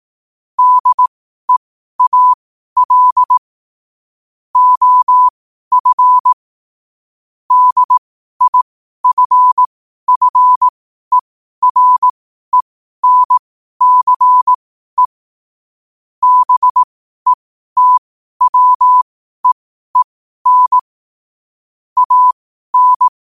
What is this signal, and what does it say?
Morse code, heard as a keyed tone, about 18 wpm, spaced as for 12 wpm, 1000 Hz. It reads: DEAL OF DIFFERENCE BETWEEN AN
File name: qotdfiles / 12wpm